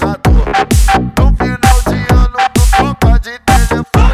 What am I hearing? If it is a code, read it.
There is no code.